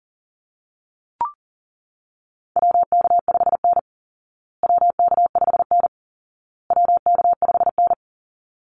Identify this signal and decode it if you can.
Morse code, heard as a keyed tone, about 40 wpm, 700 Hz. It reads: WK5N WK5N WK5N
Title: Call Signs Repeated 2x - 40wpm